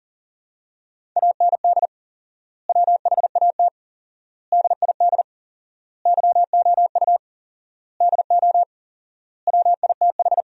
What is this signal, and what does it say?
Morse code, heard as a keyed tone, about 37 wpm, 700 Hz. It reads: AND WHAT DID YOU DO WITH
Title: 4x Word-Spacing - Sentences from Top 300 Words - 40wpm